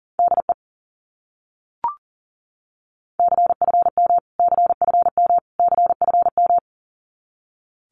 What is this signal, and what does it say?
Morse code, heard as a keyed tone, about 40 wpm, 700 Hz. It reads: DE CFM CFM CFM
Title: Abbreviations Repeated 3x - 40wpm